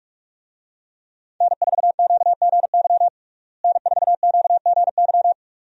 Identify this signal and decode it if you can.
Morse code, heard as a keyed tone, about 46 wpm, 700 Hz. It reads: N4QGY N4QGY
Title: USA Call Signs - 5 Characters - Repeated 2x - 45wpm